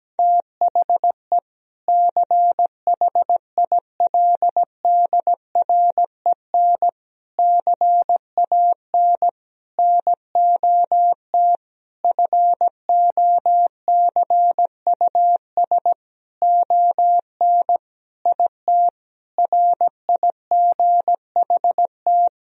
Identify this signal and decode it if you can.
Morse code, heard as a keyed tone, about 17 wpm, 700 Hz. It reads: THE CHILDREN CAN NOT FOCUS ON IT RIGHT
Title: Sentences from Top 1000 Words - Review Part 4 - 17wpm